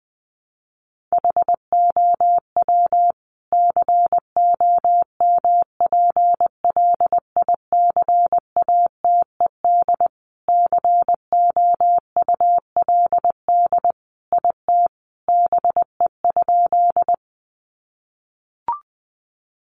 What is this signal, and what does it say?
Morse code, heard as a keyed tone, about 19 wpm, 700 Hz. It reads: HOW COMPLICATED COULD IT BE?
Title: Sentences from Top 1200 Words 20wpm